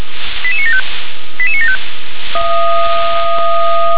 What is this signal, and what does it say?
Five-tone selective call (ZVEI), heard as quick runs of five tones, tones of about 70 ms. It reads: 88975 88975
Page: SDR/ZVEI